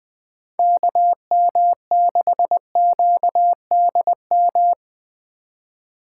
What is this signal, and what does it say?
Morse code, heard as a keyed tone, about 20 wpm, 700 Hz. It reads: KM6QDM
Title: USA Call Signs - 4x Character Spacing - 20wpm